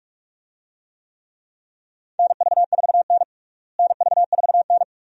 Morse code, heard as a keyed tone, about 46 wpm, 700 Hz. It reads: NV4N NV4N
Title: USA Call Signs - 4 Characters - Repeated 2x - 45wpm